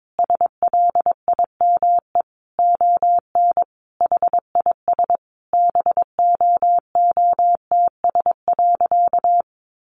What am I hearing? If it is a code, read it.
Morse code, heard as a keyed tone, about 22 wpm, 700 Hz. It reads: SLIME ON HIS BOOTS.
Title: A Sound Of Thunder 22wpm